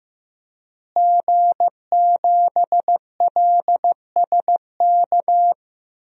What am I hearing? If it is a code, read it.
Morse code, heard as a keyed tone, about 15 wpm, 700 Hz. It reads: G7LSK
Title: DXCC-100 Call Signs - 5 Characters - Repeated 2x - 15wpm